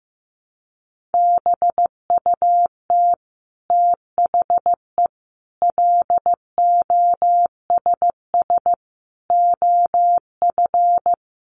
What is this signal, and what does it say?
Morse code, heard as a keyed tone, about 14 wpm, 700 Hz. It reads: BUT THE LOSS OF
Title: Sentences from Top 1400 Words 15wpm